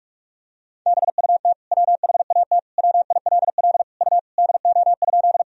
Morse code, heard as a keyed tone, about 46 wpm, 700 Hz. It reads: BUT WHAT WILL U DO?
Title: Sentences from Top 100 Words and Abbreviations 45wpm